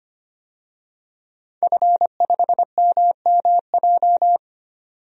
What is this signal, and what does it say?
Morse code, heard as a keyed tone, about 25 wpm, 700 Hz. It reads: F5MMJ
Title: DXCC-100 Call Signs - 5 Characters - 25wpm